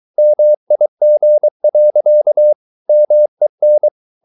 Morse code, heard as a keyed tone, about 23 wpm, 600 Hz. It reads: MIG. MEN
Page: Ferie 23 wpm